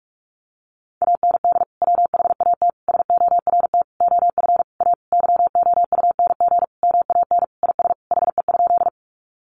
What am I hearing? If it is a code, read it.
Morse code, heard as a keyed tone, about 44 wpm, 700 Hz. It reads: AND WHAT SORT OF A YOUNG MAN IS HE?
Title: Sentences from Top 1300 Words 45wpm